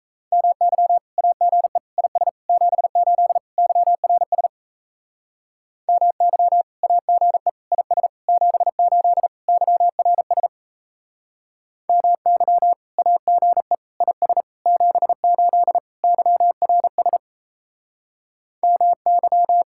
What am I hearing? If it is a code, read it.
Morse code, heard as a keyed tone, about 36 wpm, 700 Hz. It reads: MY AGE IS 78 YRS MY AGE IS 78 YRS MY AGE IS 78 YRS MY